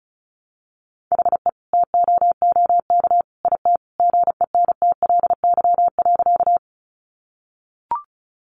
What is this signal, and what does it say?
Morse code, heard as a keyed tone, about 34 wpm, 700 Hz. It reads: HE TOOK IT GENTLY.
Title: Sentences from Top 1500 Words 35wpm